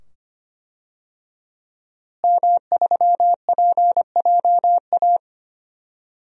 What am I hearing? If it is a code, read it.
Morse code, heard as a keyed tone, about 25 wpm, 700 Hz. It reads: M3PJA
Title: DXCC-100 Call Signs - 5 Characters - Repeated 2x - 25wpm